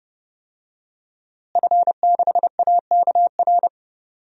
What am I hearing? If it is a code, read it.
Morse code, heard as a keyed tone, about 31 wpm, 700 Hz. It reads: F6AKR
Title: DXCC-100 Call Signs - 5 Characters - Repeated 2x - 30wpm